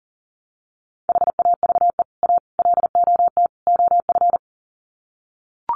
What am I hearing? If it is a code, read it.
Morse code, heard as a keyed tone, about 40 wpm, 700 Hz. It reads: HAVE A LOT OF
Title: Sets of 4 Words 40wpm